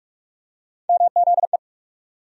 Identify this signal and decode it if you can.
Morse code, heard as a keyed tone, about 42 wpm, 700 Hz. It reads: MZE
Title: USA Call Sign Postfixes - 3 Letters 45wpm